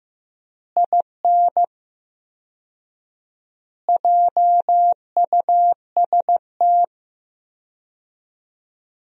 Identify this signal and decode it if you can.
Morse code, heard as a keyed tone, about 15 wpm, 700 Hz. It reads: IN JUST
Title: 4x Word-Spacing - Sentences from Top 300 Words - 15wpm